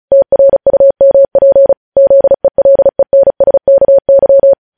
Morse code, heard as a keyed tone, about 36 wpm, 550 Hz. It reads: TRUMP ZELENSKY
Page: News Headlines in Morse Code at 35 WPM